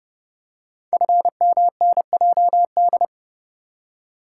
Morse code, heard as a keyed tone, about 29 wpm, 700 Hz. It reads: FMNJD
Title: Code Groups - 5 Letters - 30wpm